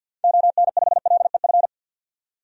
Morse code, heard as a keyed tone, about 50 wpm, 700 Hz. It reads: ON5LEF
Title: DXCC-100 Call Signs - 6 Characters - 50wpm